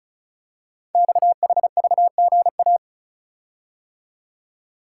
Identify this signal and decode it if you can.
Morse code, heard as a keyed tone, about 36 wpm, 690 Hz. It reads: XHVGA
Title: Code Groups - 5 Letters - 35wpm